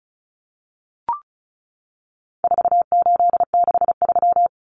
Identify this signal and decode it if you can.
Morse code, heard as a keyed tone, about 34 wpm, 700 Hz. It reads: 4863
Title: Common Cut Numbers - 4-Digit 35wpm